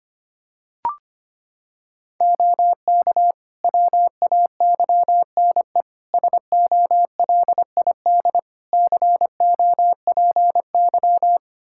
Morse code, heard as a keyed tone, about 25 wpm, 700 Hz. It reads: OK WAYNE SOLID COPY
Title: Practice QSO - Exchange 3 - More Info - 25wpm